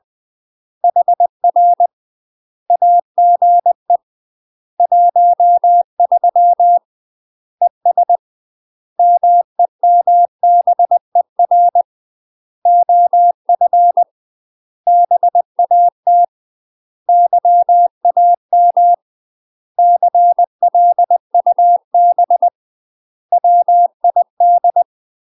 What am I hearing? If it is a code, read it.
Morse code, heard as a keyed tone, about 20 wpm, 700 Hz. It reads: HR AGE 13 ES MEMBER OF BAT YAM CLUB WID
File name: UK MASTER QSOs TRAINING / 20WPM / UK MASTER QSOs 0036